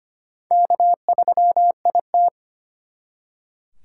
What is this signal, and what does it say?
Morse code, heard as a keyed tone, about 32 wpm, 700 Hz. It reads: K3IT